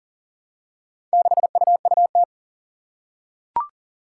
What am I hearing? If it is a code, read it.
Morse code, heard as a keyed tone, about 42 wpm, 700 Hz. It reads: 6UUT
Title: Four Challenging Letter-Number Combinations 40wpm